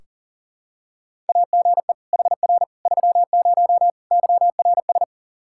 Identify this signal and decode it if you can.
Morse code, heard as a keyed tone, about 40 wpm, 690 Hz. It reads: AGE HR 30 YRS
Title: Practice QSO - Exchange 3 - More Info - 40wpm